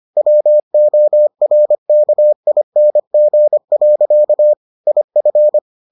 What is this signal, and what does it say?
Morse code, heard as a keyed tone, about 25 wpm, 600 Hz. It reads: WORKING. IF